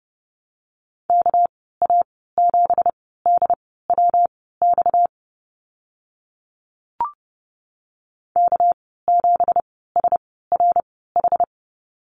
Morse code, heard as a keyed tone, about 30 wpm, spaced as for 17 wpm, 700 Hz. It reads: KA7DWX K7SRH
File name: Lesson 37g - Letter x - Call Signs 30wpm